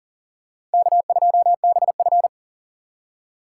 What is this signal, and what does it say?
Morse code, heard as a keyed tone, about 37 wpm, 700 Hz. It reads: K2BF
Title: USA Call Signs - 4 Characters - 3x Character Spacing - 40wpm